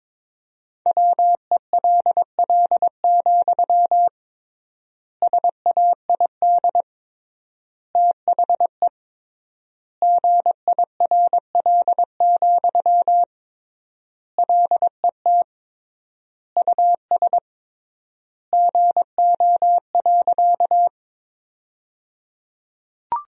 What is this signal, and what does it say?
Morse code, heard as a keyed tone, about 22 wpm, 700 Hz. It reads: WELL, SAID THE GIRL, LET US GO.
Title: 3x Word-Spacing - Sentences from Top 400 Words - 22wpm